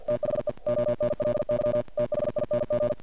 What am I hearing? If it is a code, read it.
Morse code, heard as a keyed tone, about 48 wpm, 590 Hz, with noise. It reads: THE ONLY THING
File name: archived / CRSnet